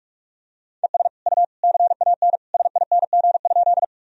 Morse code, heard as a keyed tone, about 46 wpm, 700 Hz. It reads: ES U CAN SING?
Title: Sentences from Top 400 Words and Abbreviations 45wpm